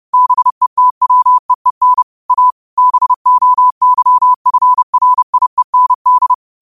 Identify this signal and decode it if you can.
Morse code, heard as a keyed tone, about 30 wpm, 1000 Hz. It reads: BETWEEN A BOYFRIEND
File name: qotdfiles / 30wpm